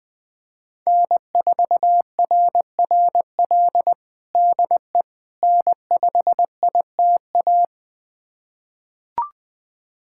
Morse code, heard as a keyed tone, about 21 wpm, 700 Hz. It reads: N4RRL DE N5ITA
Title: USA Callsign DE Callsign - 5 Characters - 20wpm